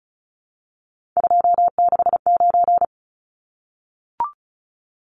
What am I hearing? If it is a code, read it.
Morse code, heard as a keyed tone, about 34 wpm, 700 Hz. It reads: 269
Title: Numbers - 3-Digit 35wpm